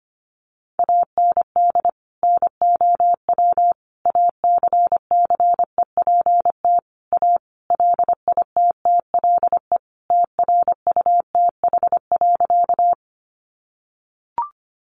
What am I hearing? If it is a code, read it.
Morse code, heard as a keyed tone, about 25 wpm, 700 Hz. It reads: AND NOW ACCEPT A LITTLE TRUTH.
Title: Sentences from Top 1300 Words 25wpm